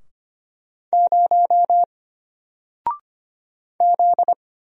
Morse code, heard as a keyed tone, about 25 wpm, 700 Hz. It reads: 0 Z
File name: Single-Letter-Number-25wpm